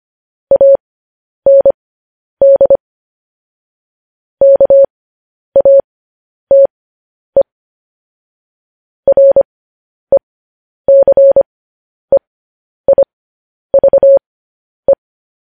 Morse code, heard as a keyed tone, about 25 wpm, spaced as for 10 wpm, 550 Hz. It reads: AND KATE RECEIVE